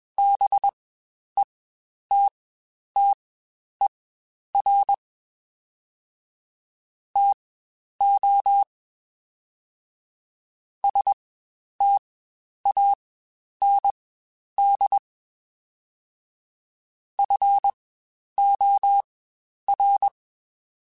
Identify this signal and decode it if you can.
Morse code, heard as a keyed tone, about 21 wpm, spaced as for 10 wpm, 800 Hz. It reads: BETTER TO STAND FOR
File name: Quotes Spaced / Quotes Spaced-mp3s / StandFor sp @20